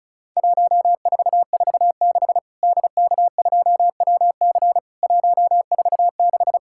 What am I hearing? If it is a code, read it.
Morse code, heard as a keyed tone, about 36 wpm, 700 Hz. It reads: 1446 DK2WC 146
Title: Worked All Europe DX Contest - QTC Exchange - 35wpm